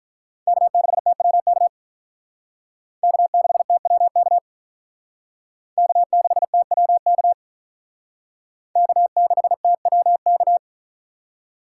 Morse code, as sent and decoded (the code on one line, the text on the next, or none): -.- -.... - .-- -.- / -.- -.... - .-- -.- / -.- -.... - .-- -.- / -.- -.... - .-- -.-
K6TWK K6TWK K6TWK K6TWK